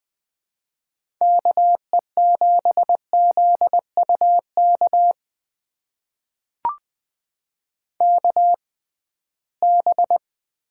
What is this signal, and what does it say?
Morse code, heard as a keyed tone, about 20 wpm, spaced as for 15 wpm, 700 Hz. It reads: KE7ZUK KB
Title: USA Call Signs - 6 Characters - 6x Character Spacing - 20wpm